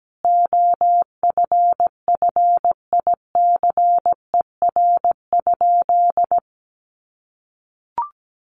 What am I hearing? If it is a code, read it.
Morse code, heard as a keyed tone, about 17 wpm, 700 Hz. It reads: OFFICER?
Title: Sentences from Top 1200 Words 17wpm